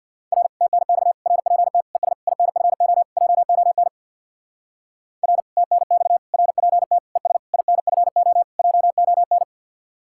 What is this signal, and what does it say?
Morse code, heard as a keyed tone, about 64 wpm, 700 Hz. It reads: R TNX RPT ES INFO JON R TNX RPT ES INFO JON